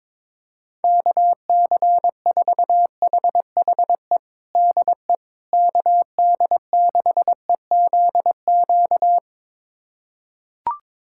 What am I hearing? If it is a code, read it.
Morse code, heard as a keyed tone, about 22 wpm, 700 Hz. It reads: KC4HHE DE KD6EZQ
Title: USA Callsign DE Callsign - 6 Characters - 22wpm